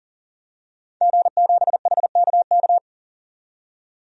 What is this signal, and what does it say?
Morse code, heard as a keyed tone, about 40 wpm, 700 Hz. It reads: G7HKK
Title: DXCC-100 Call Signs - 5 Characters - 40wpm